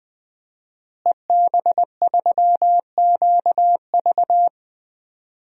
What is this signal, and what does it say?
Morse code, heard as a keyed tone, about 20 wpm, 700 Hz. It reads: EB3QV
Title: DXCC-100 Call Signs - 5 Characters - Repeated 3x - 20wpm